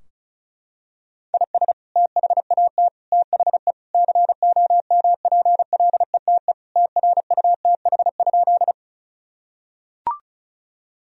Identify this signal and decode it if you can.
Morse code, heard as a keyed tone, about 36 wpm, 700 Hz. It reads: IS THAT THE COMPLETE TRUTH?
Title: Sentences from Top 1000 Words - Review Part 2 - 35wpm